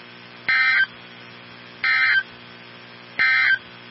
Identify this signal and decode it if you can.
Emergency Alert System (SAME) tones, heard as harsh buzzing bursts. It is heard from 0.4 to 3.7 s.
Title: NOAA Weather Radio 162.550